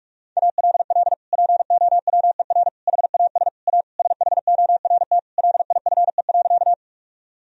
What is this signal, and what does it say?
Morse code, heard as a keyed tone, about 42 wpm, 700 Hz. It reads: ALL POWER HAS A SHORT LIFE.
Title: Sentences from Top 1000 Words - Review Part 2 - 45wpm